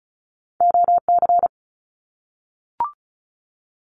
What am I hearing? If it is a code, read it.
Morse code, heard as a keyed tone, about 34 wpm, 700 Hz. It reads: OC
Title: Abbreviations Repeated 3x - 35wpm